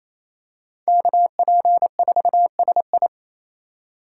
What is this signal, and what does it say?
Morse code, heard as a keyed tone, about 27 wpm, 700 Hz. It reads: KP4SI
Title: USA Call Signs - 5 Characters - 4x Character Spacing - 28wpm